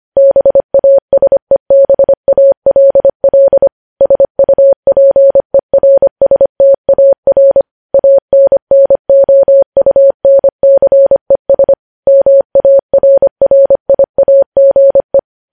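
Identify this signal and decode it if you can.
Morse code, heard as a keyed tone, about 25 wpm, 550 Hz. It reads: BASEBALL SUPERSTAR ANNOUNCES MARRIAGE
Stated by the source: Morse news headlines sent at 25 wpm.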